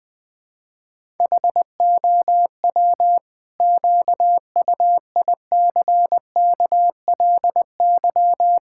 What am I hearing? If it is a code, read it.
Morse code, heard as a keyed tone, about 20 wpm, 700 Hz. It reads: HOW QUICKLY
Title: Sentences from Top 1000 Words - Review Part 5 - 20wpm